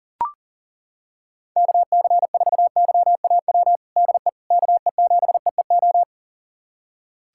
Morse code, heard as a keyed tone, about 37 wpm, 700 Hz. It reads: KC4YAW DE KE7EEO
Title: USA Callsign DE Callsign 40wpm